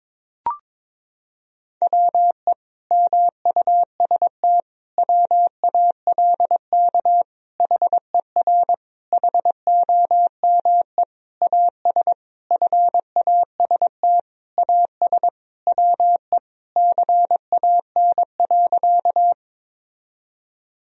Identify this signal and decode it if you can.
Morse code, heard as a keyed tone, about 22 wpm, 700 Hz. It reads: WE MUST WALK HER HOME AS FAST AS WE CAN.
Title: Sentences from Top 1000 Words - Review Part 2 - 22wpm